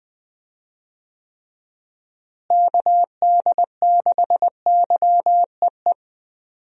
Morse code, heard as a keyed tone, about 20 wpm, 700 Hz. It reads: KD6YEE
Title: USA Call Signs - 6 Characters - Repeated 3x - 20wpm